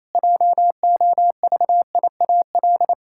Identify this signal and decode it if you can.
Morse code, heard as a keyed tone, about 28 wpm, 700 Hz. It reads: JOVIAL
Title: Pangram Challenge v2 - 28wpm